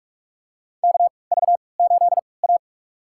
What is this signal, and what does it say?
Morse code, heard as a keyed tone, about 42 wpm, spaced as for 25 wpm, 700 Hz. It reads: KV8A
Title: USA Call Signs - 4 Characters - 3x Character Spacing - 45wpm